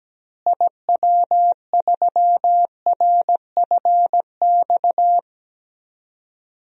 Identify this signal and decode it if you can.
Morse code, heard as a keyed tone, about 17 wpm, 700 Hz. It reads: IW3RFX
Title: DXCC-100 Call Signs - 6 Characters - Repeated 2x - 17wpm